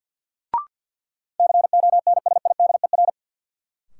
Morse code, heard as a keyed tone, about 47 wpm, 700 Hz. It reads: CONSIDER
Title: Sentences from Top 600 Words - New Word Drill - 50wpm